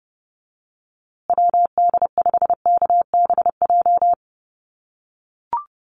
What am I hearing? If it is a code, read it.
Morse code, heard as a keyed tone, about 30 wpm, 700 Hz. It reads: WD5KBJ
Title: USA Call Signs Repeated 2x 30wpm